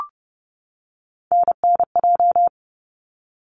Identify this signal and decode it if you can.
Morse code, heard as a keyed tone, about 30 wpm, 700 Hz. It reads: NNJ
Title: ARRL Sections 30wpm